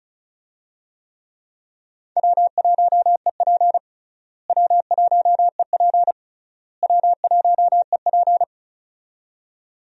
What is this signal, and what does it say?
Morse code, heard as a keyed tone, about 36 wpm, 700 Hz. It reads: W1EP W1EP W1EP
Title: USA Call Signs - 4 Characters - Repeated 3x - 35wpm